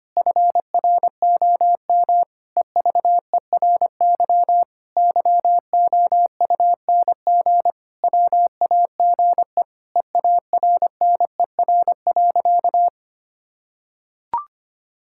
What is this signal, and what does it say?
Morse code, heard as a keyed tone, about 25 wpm, 700 Hz. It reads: FROM EVERY YOUNG WAGE EARNER.
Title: Pangram Challenge v2 - 25wpm